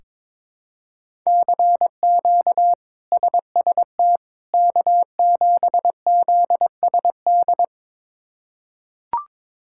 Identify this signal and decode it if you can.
Morse code, heard as a keyed tone, about 21 wpm, 700 Hz. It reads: CQ SST K7ZSD
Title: K1USN SST QSO Practice - 22wpm